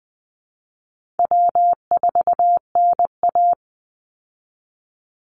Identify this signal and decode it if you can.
Morse code, heard as a keyed tone, about 20 wpm, 700 Hz. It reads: W4NA
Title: Call Signs Repeated 3x - 20wpm